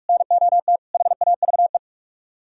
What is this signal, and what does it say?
Morse code, heard as a keyed tone, about 60 wpm, spaced as for 52 wpm, 700 Hz. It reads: NOT HAVE
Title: Speed-Racing - Sets of 3 Words 45wpm